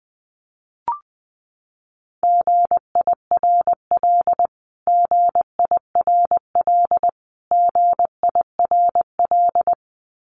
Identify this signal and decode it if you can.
Morse code, heard as a keyed tone, about 20 wpm, 700 Hz. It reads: GIRL GIRL GIRL
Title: Top 500 Words Repeated 20wpm